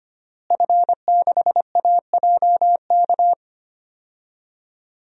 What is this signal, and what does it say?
Morse code, heard as a keyed tone, about 25 wpm, 700 Hz. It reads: F6AJK
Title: DXCC-100 Call Signs - 5 Characters - Repeated 3x - 25wpm